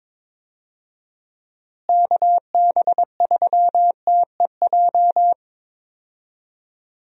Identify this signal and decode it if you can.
Morse code, heard as a keyed tone, about 22 wpm, 700 Hz. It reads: KB3TEJ
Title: USA Call Signs - 6 Characters - Repeated 2x - 22wpm